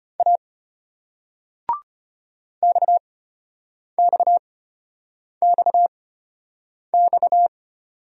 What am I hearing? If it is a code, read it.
Morse code, heard as a keyed tone, about 33 wpm, spaced as for 8 wpm, 700 Hz. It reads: A XXXX